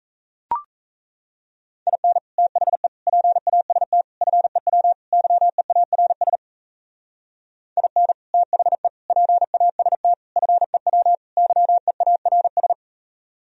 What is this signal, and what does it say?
Morse code, heard as a keyed tone, about 36 wpm, 700 Hz. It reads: IN THE PAST FEW YEARS IN THE PAST FEW YEARS